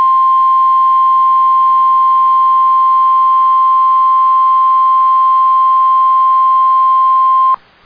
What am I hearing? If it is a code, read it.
Emergency Alert System (SAME) tones (the 1050 Hz attention tone). It is heard from 0.5 to 6.9 s.